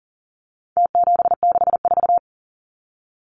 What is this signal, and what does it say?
Morse code, heard as a keyed tone, about 40 wpm, 700 Hz. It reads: T764
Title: Numbers - Contest Style - 4-Digit 40wpm